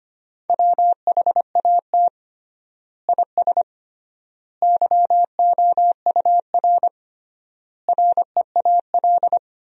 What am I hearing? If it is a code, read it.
Morse code, heard as a keyed tone, about 25 wpm, 700 Hz. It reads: WHAT IS YOUR REAL
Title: 3x Word-Spacing - Sentences from Top 300 Words - 25wpm